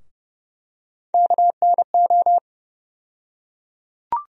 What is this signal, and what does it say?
Morse code, heard as a keyed tone, about 32 wpm, 700 Hz. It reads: KNO
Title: USA Call Sign Postfixes - 3 Letters 30wpm